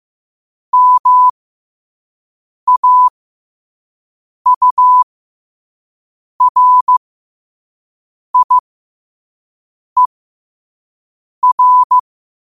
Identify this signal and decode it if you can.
Morse code, heard as a keyed tone, about 15 wpm, spaced as for 5 wpm, 1000 Hz. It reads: MAURIER
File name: qotdfiles / 5wpm